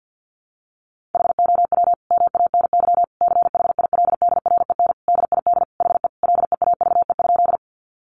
Morse code, heard as a keyed tone, about 50 wpm, 700 Hz. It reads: HOW MANY CHILDREN DID HE LEAVE?
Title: Sentences from Top 400 Words 50wpm